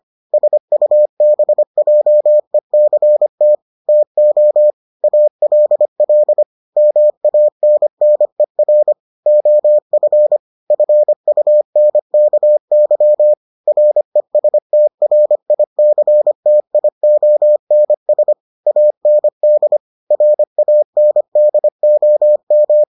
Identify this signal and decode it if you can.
Morse code, heard as a keyed tone, about 25 wpm, 600 Hz. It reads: SUBJECT TO ALL MANNER OF FUNKY RESTRICTIONS AND RANDOM